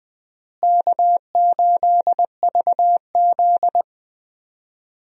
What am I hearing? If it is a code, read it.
Morse code, heard as a keyed tone, about 20 wpm, 700 Hz. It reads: K8VZ
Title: USA Call Signs - 4 Characters - Repeated 3x - 20wpm